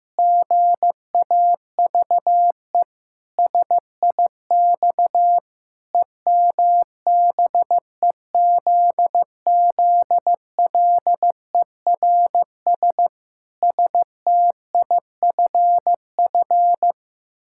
Morse code, heard as a keyed tone, about 16 wpm, 700 Hz. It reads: GAVE SIX EMBEZZLERS STIFF
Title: Pangram Challenge v2 - 15wpm